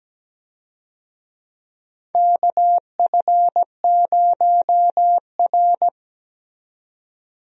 Morse code, heard as a keyed tone, about 17 wpm, 700 Hz. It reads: KF0R
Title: USA Call Signs - 4 Characters - 17wpm